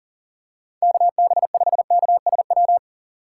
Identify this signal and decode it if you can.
Morse code, heard as a keyed tone, about 40 wpm, 700 Hz. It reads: KB5KSW
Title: USA Call Signs - 6 Characters - 3x Character Spacing - 40wpm